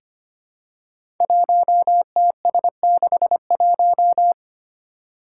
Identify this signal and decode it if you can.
Morse code, heard as a keyed tone, about 25 wpm, 700 Hz. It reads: 1TS61
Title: Five Challenging Letter-Number Combinations 25wpm